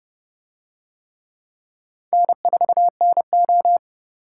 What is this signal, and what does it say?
Morse code, heard as a keyed tone, about 29 wpm, 700 Hz. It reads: N4NO